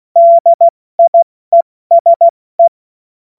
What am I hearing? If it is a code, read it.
Morse code, heard as a keyed tone, about 16 wpm, 680 Hz. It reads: DIESE
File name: Morsekurs / Witze / Witz 17. Mai 2022 70 F80